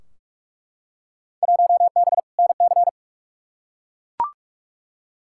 Morse code, heard as a keyed tone, about 42 wpm, 700 Hz. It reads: 1B NC
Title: Field Day Exchanges - 45wpm